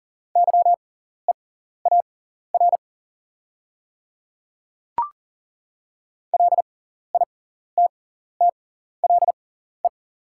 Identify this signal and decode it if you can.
Morse code, heard as a keyed tone, about 40 wpm, spaced as for 14 wpm, 690 Hz. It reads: YEAR LITTLE
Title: 6x Character Spacing - Top 300 Words - 40wpm